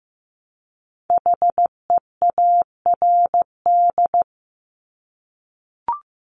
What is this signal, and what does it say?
Morse code, heard as a keyed tone, about 15 wpm, 700 Hz. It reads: HEARD
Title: Multi-letter Phonemes Repeated 3x 15wpm